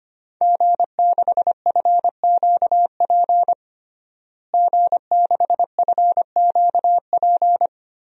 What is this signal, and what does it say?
Morse code, heard as a keyed tone, about 25 wpm, 700 Hz. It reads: G6FQP G6FQP
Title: DXCC-100 Call Signs - 5 Characters - Repeated 3x - 25wpm